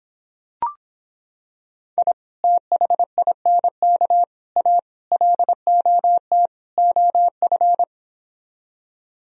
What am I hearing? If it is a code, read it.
Morse code, heard as a keyed tone, about 26 wpm, 700 Hz. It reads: I THINK A LOT OF